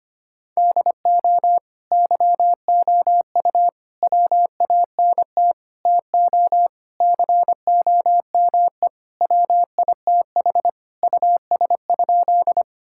Morse code, heard as a keyed tone, about 25 wpm, 700 Hz. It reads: DO YOU WANT TO COME WITH US?
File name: practice / 25wpm / Sentences with 3-Word Phrases - 25wpm